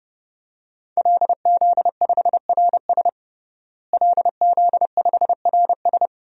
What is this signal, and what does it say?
Morse code, heard as a keyed tone, about 32 wpm, 700 Hz. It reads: LZ5RS LZ5RS
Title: DXCC-100 Call Signs - 5 Characters - Repeated 2x - 30wpm